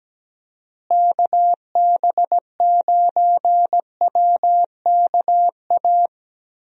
Morse code, heard as a keyed tone, about 17 wpm, 700 Hz. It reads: KB9WKA
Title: USA Call Signs - 6 Characters - Repeated 3x - 17wpm